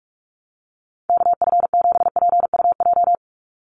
Morse code, heard as a keyed tone, about 44 wpm, 700 Hz. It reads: KF7PUJ
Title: USA Call Signs 45wpm